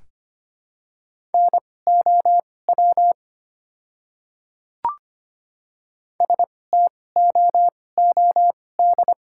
Morse code, heard as a keyed tone, about 25 wpm, spaced as for 18 wpm, 700 Hz. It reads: NOW STOOD
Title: 2x Character Spacing - Top 500 Words - 25wpm